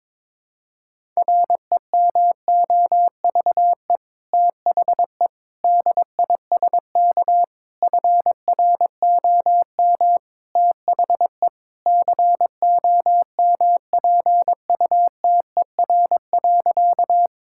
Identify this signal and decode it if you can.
Morse code, heard as a keyed tone, about 22 wpm, 700 Hz. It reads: REMOVE THE DISK FROM THE COMPUTER.
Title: Sentences from Top 1000 Words - Review Part 6 - 22wpm